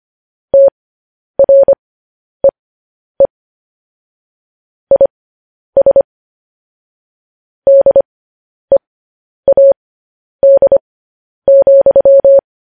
Morse code, heard as a keyed tone, about 25 wpm, spaced as for 10 wpm, 550 Hz. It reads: TREE IS DEAD,
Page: news-headlines-in-morse-code-at-10-wpm